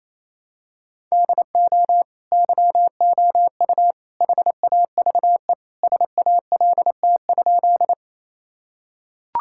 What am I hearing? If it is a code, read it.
Morse code, heard as a keyed tone, about 28 wpm, 700 Hz. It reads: DO YOU HAVE SALT?
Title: Sentences from Top 1000 Words - Review Part 5 - 28wpm